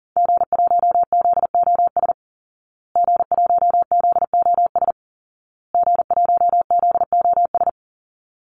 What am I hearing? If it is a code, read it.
Morse code, heard as a keyed tone, about 40 wpm, 700 Hz. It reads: G1ZOS G1ZOS G1ZOS
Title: Call Signs Repeated 2x - 40wpm